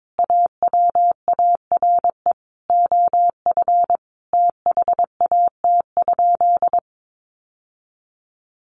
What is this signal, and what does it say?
Morse code, heard as a keyed tone, about 22 wpm, 700 Hz. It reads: AWARE OF THAT?
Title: Sentences from Top 1200 Words 22wpm